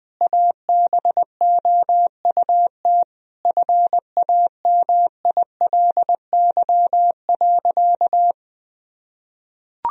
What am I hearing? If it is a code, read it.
Morse code, heard as a keyed tone, about 21 wpm, 700 Hz. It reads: ABOUT FAMILY.
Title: Sentences from Top 1000 Words - Review Part 3 - 20wpm